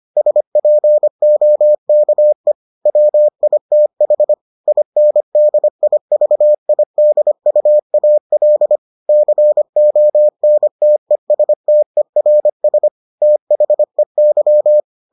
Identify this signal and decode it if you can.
Morse code, heard as a keyed tone, about 25 wpm, 600 Hz. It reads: SPOKE WITH INDIVIDUAL CONTESTERS THEY